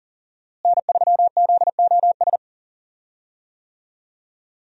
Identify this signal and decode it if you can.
Morse code, heard as a keyed tone, about 40 wpm, 700 Hz. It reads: N3ZOS
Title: USA Call Signs - 5 Characters - Repeated 2x - 40wpm